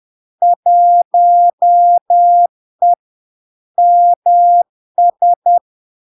Morse code, heard as a keyed tone, about 10 wpm, 700 Hz. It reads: 1E MS
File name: LARK-FD-2022-Log-in-CW-10wpm-700-Hz